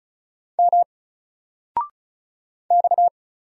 Morse code, heard as a keyed tone, about 36 wpm, 700 Hz. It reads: M X
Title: Single Letter-Number 35wpm